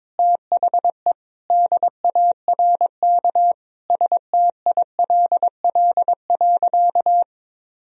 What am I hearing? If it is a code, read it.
Morse code, heard as a keyed tone, about 22 wpm, 700 Hz. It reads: THE DARK STILL.
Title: Sentences from Top 1000 Words - Review Part 4 - 22wpm